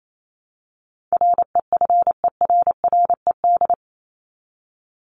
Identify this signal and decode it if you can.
Morse code, heard as a keyed tone, about 28 wpm, 700 Hz. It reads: REFERRED
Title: ED Suffix - Top 100 Words with Base Words 28wpm